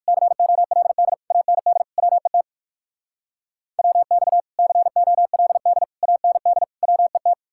Morse code, heard as a keyed tone, about 64 wpm, spaced as for 55 wpm, 700 Hz. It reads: COLD AND WET WX COLD AND WET
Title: Speed-Racing Common QSO Phrases Encore - 45wpm